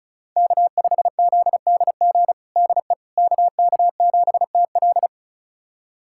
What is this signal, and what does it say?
Morse code, heard as a keyed tone, about 36 wpm, 700 Hz. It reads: K5ZDG DE KK7TL
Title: USA Callsign DE Callsign - 5 Characters - 35wpm